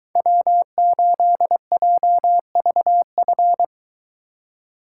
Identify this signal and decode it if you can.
Morse code, heard as a keyed tone, about 21 wpm, 700 Hz. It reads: W8JVF